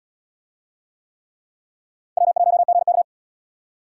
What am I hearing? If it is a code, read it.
Morse code, heard as a keyed tone, about 77 wpm, 700 Hz. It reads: W1GK